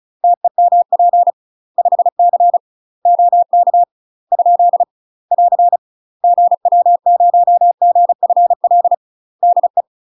Morse code, heard as a keyed tone, about 34 wpm, 700 Hz. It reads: TEMP 5C OK ? <AR> GW0GFL DE
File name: UK MASTER QSOs TRAINING / 35WPM / UK MASTER QSOs 0030